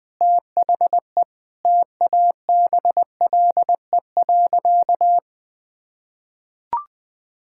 Morse code, heard as a keyed tone, about 20 wpm, 700 Hz. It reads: THE TABLE.
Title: Sentences from Top 1000 Words - Review Part 6 - 20wpm